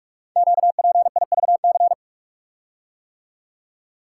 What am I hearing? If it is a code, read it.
Morse code, heard as a keyed tone, about 44 wpm, 700 Hz. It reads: QPIVC
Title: Code Groups - 5 Letters - 45wpm